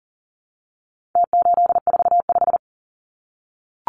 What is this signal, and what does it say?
Morse code, heard as a keyed tone, about 38 wpm, 700 Hz. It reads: T845
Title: Numbers - Contest Style - 4-Digit 40wpm